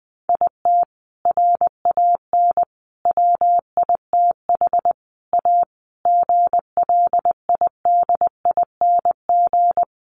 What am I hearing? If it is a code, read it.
Morse code, heard as a keyed tone, about 19 wpm, 700 Hz. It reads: IT RAN WITH A GLIDING
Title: A Sound Of Thunder 20wpm